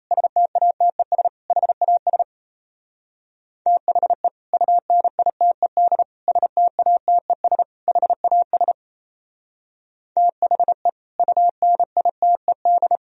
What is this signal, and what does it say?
Morse code, heard as a keyed tone, about 32 wpm, 700 Hz. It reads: STATES HAS THE UNITED STATES HAS THE UNITED
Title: Speed-Racing - Sets of 4 Words 28wpm